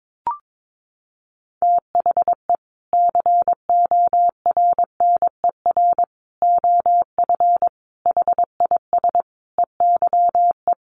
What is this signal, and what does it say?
Morse code, heard as a keyed tone, about 22 wpm, 700 Hz. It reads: THE CORNER OF HIS EYE
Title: Sets of 5 Words Encore2 22wpm